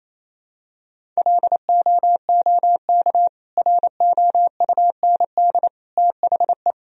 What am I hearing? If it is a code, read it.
Morse code, heard as a keyed tone, about 28 wpm, 700 Hz. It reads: LOOK ROUND THE
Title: Sentences from Top 1000 Words - Review Part 1 - 28wpm